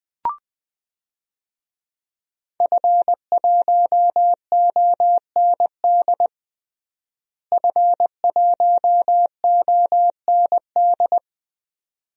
Morse code, heard as a keyed tone, about 20 wpm, 700 Hz. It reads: F1OND F1OND
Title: DXCC-100 Call Signs - 5 Characters - Repeated 2x - 20wpm